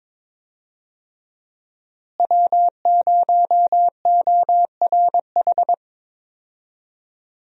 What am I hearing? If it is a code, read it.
Morse code, heard as a keyed tone, about 22 wpm, 700 Hz. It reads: W0ORH
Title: USA Call Signs - 5 Characters - Repeated 3x - 22wpm